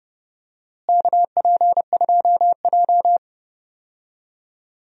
Morse code, heard as a keyed tone, about 31 wpm, 700 Hz. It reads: KP2J
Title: DXCC-100 Call Signs - 4 Characters - Repeated 2x - 30wpm